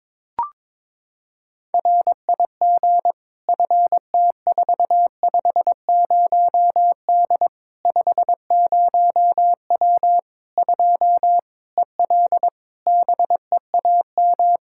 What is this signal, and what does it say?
Morse code, heard as a keyed tone, about 22 wpm, 700 Hz. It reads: RIG FT450D 50W 2 EL BEAM
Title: Practice QSO - Exchange 2 - Info - 22wpm